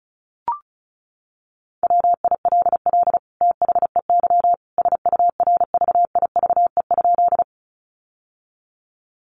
Morse code, heard as a keyed tone, about 34 wpm, 700 Hz. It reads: WILL THEY SURVIVE?
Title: Sentences from Top 1300 Words 35wpm